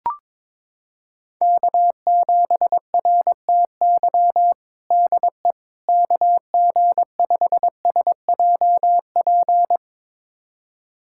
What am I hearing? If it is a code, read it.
Morse code, heard as a keyed tone, about 22 wpm, 700 Hz. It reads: K7RTY DE KG5SJP
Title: USA Callsign DE Callsign 22wpm